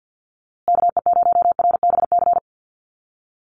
Morse code, heard as a keyed tone, about 50 wpm, 700 Hz. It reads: KE0RBC
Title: USA Call Signs Repeated 2x 50wpm